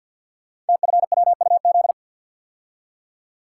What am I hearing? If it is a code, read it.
Morse code, heard as a keyed tone, about 46 wpm, 700 Hz. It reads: TFWUZ
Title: Code Groups - 5 Letter:Numbers - 50wpm